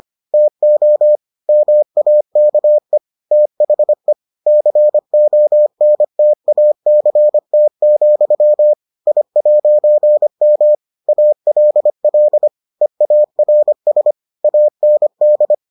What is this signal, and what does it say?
Morse code, heard as a keyed tone, about 25 wpm, 600 Hz. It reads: TO MAKE THE CONTACT, I'M ALL EARS AND